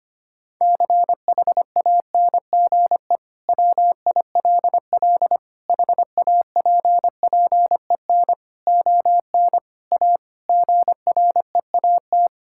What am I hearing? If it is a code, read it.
Morse code, heard as a keyed tone, about 25 wpm, 700 Hz. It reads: CHANGE WILL HAPPEN ON A GREAT
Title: Sentences from Top 1000 Words - Review Part 5 - 25wpm